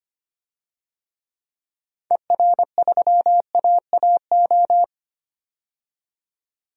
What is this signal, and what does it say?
Morse code, heard as a keyed tone, about 25 wpm, 700 Hz. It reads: ER3AAO
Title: DXCC-100 Call Signs - 6 Characters - 25wpm